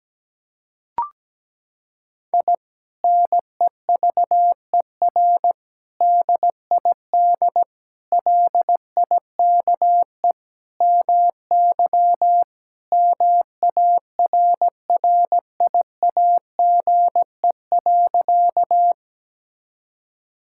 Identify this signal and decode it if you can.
Morse code, heard as a keyed tone, about 17 wpm, 700 Hz. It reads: I NEVER DID LIKE MY MARRIAGE.
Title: Sentences from Top 1000 Words - Review Part 5 - 17wpm